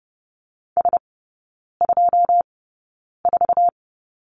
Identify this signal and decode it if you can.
Morse code, heard as a keyed tone, about 30 wpm, spaced as for 9 wpm, 700 Hz. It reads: S24
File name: 081 Lesson 14e - Number 2 - Three Characters 30wpm